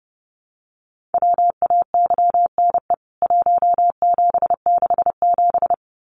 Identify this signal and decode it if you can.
Morse code, heard as a keyed tone, about 30 wpm, 700 Hz. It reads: WAYNE 1767
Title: CWT Exchange 30wpm